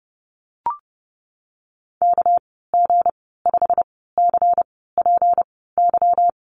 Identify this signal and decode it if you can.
Morse code, heard as a keyed tone, about 30 wpm, spaced as for 17 wpm, 700 Hz. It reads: KG5CPY
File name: Lesson 37g - Letter x - Call Signs 30wpm